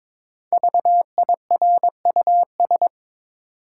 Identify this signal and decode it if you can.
Morse code, heard as a keyed tone, about 22 wpm, 700 Hz. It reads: VIRUS
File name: 5 Letter Words - First 3 Letters Given - 22wpm